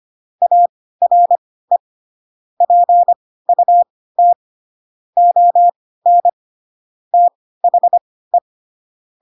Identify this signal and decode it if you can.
Morse code, heard as a keyed tone, about 25 wpm, spaced as for 16 wpm, 700 Hz. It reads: ARE PUT ON THE
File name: Pinocchio / Pinocchio ch1 10 25 ik1hge